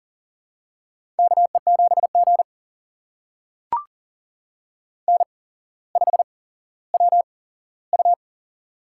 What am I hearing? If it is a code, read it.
Morse code, heard as a keyed tone, about 40 wpm, spaced as for 17 wpm, 700 Hz. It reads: KE7G N5WU
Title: USA Call Signs - 8x Character Spacing - 40wpm